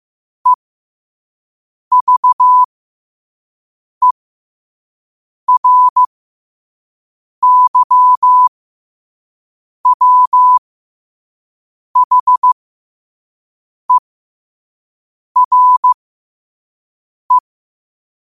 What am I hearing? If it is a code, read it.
Morse code, heard as a keyed tone, about 15 wpm, spaced as for 5 wpm, 1000 Hz. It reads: EVERYWHERE
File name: qotdfiles / 5wpm